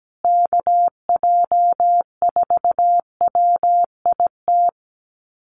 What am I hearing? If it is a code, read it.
Morse code, heard as a keyed tone, about 17 wpm, 700 Hz. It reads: KJ4WIT
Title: USA Call Signs Repeated 3x 17wpm